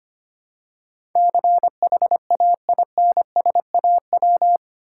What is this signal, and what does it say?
Morse code, heard as a keyed tone, about 25 wpm, 700 Hz. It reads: CHAINSAW
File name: practice / 25wpm / Difficult Words with Hints - 25wpm